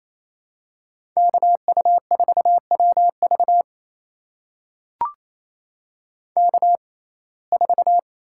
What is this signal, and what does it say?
Morse code, heard as a keyed tone, about 28 wpm, spaced as for 20 wpm, 700 Hz. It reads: KU4WV K4
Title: USA Call Signs - 5 Characters - 6x Character Spacing - 28wpm